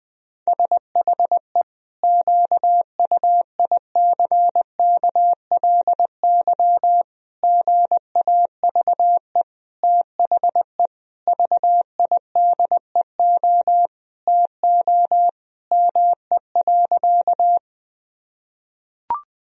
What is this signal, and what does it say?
Morse code, heard as a keyed tone, about 21 wpm, 700 Hz. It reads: SHE QUICKLY GAVE THE VIDEO TO ME.
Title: Sentences from Top 1000 Words - Review Part 4 - 20wpm